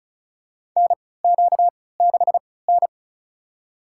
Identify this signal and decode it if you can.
Morse code, heard as a keyed tone, about 36 wpm, spaced as for 20 wpm, 700 Hz. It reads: NQ6N
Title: USA Call Signs - 4 Characters - 3x Character Spacing - 35wpm